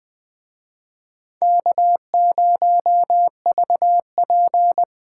Morse code, heard as a keyed tone, about 23 wpm, 700 Hz. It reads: K0VP
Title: Speed Racing - USA 1x2 Call Signs 20wpm